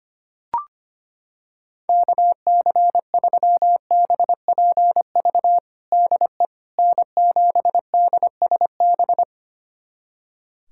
Morse code, heard as a keyed tone, about 25 wpm, 700 Hz. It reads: KC3BPV DE N7DSB
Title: USA Callsign DE Callsign 25wpm